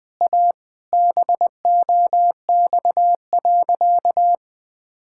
Morse code, heard as a keyed tone, about 21 wpm, 700 Hz. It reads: A BOX.
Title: Sentences with 3-Word Phrases - 20wpm